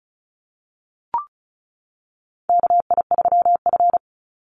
Morse code, heard as a keyed tone, about 34 wpm, 700 Hz. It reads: KI3F
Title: Call Signs - 35wpm